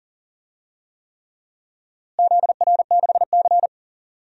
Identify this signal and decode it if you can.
Morse code, heard as a keyed tone, about 40 wpm, 700 Hz. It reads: ZR6C
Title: DXCC-100 Call Signs - 4 Characters - Repeated 3x - 40wpm